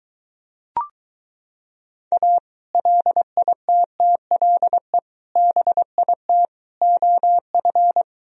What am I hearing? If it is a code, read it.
Morse code, heard as a keyed tone, about 23 wpm, 700 Hz. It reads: A LITTLE BIT OF